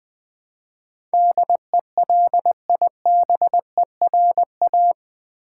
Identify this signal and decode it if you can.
Morse code, heard as a keyed tone, about 21 wpm, 700 Hz. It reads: DELIBERA
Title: Build A Word - 4 Letters At A Time - 12-Letter Words - 20wpm